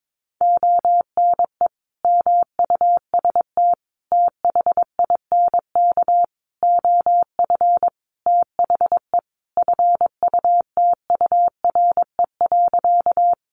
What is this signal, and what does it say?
Morse code, heard as a keyed tone, about 22 wpm, 700 Hz. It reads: ONE MUST THINK OF THE FUTURE.
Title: Sentences from Top 700 Words 22wpm